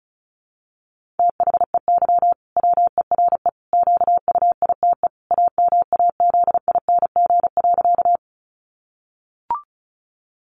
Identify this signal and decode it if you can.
Morse code, heard as a keyed tone, about 34 wpm, 700 Hz. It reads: THEY WERE QUITE AMAZING.
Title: Sentences from Top 1500 Words 35wpm